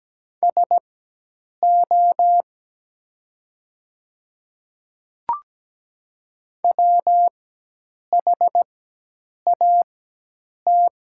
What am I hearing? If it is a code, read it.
Morse code, heard as a keyed tone, about 17 wpm, spaced as for 8 wpm, 700 Hz. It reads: SO WHAT
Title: 4x Character Spacing - Top 500 Words - 17wpm